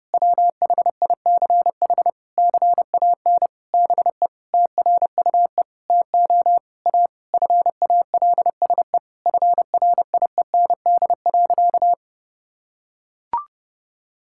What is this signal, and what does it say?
Morse code, heard as a keyed tone, about 30 wpm, 700 Hz. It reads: WHICH CAN BE TRUE TO A FALSE FRIEND.
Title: Sentences from Top 1600 Words 30wpm